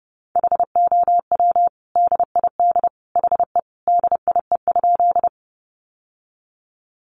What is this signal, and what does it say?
Morse code, heard as a keyed tone, about 30 wpm, 700 Hz. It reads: HOW DID HE DIE?
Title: Sentences from Top 1300 Words 30wpm